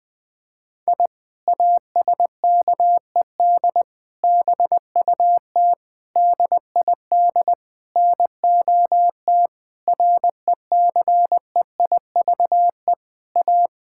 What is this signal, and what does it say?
Morse code, heard as a keyed tone, about 20 wpm, 700 Hz. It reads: I ASKED BUT DID NOT RECEIVE A
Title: Sentences with 8-Word Phrases - Spoken Beginning - 20wpm